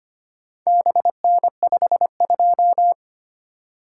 Morse code, heard as a keyed tone, about 25 wpm, 700 Hz. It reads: BN52
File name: practice / 25wpm / Four Letter-Number Combinations - 25wpm